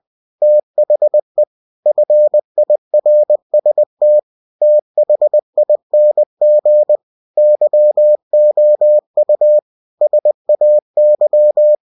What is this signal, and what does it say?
Morse code, heard as a keyed tone, about 20 wpm, 600 Hz. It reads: THE FIRST THING YOU SAY
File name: cw-books / winnie-the-pooh / winnie-20wpm-page-0110